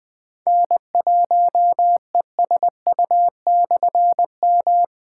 Morse code, heard as a keyed tone, about 20 wpm, 700 Hz. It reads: N1ESU/M
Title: Difficult Call Signs 20wpm